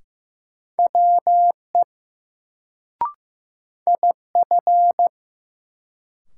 Morse code, heard as a keyed tone, about 15 wpm, 700 Hz. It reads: WE IF
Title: Sets of 2 Words - New Word Drill - 15wpm